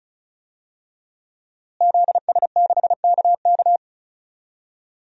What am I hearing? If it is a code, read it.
Morse code, heard as a keyed tone, about 36 wpm, 700 Hz. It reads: ZS6KK
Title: DXCC-100 Call Signs - 5 Characters - Repeated 3x - 35wpm